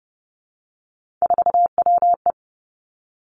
Morse code, heard as a keyed tone, about 30 wpm, 700 Hz. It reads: 4WE
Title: Three Challenging Letter-Number Combinations 30wpm